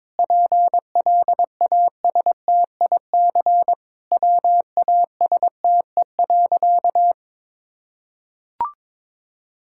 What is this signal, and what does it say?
Morse code, heard as a keyed tone, about 22 wpm, 700 Hz. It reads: PLASTIC WASTE.
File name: Difficult Sentences with Hints - 22wpm